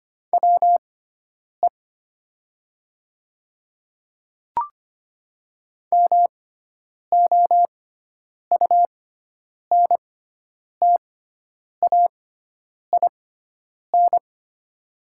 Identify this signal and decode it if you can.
Morse code, heard as a keyed tone, about 25 wpm, spaced as for 9 wpm, 700 Hz. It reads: WE MOUNTAIN
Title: 6x Character Spacing - Top 300 Words - 25wpm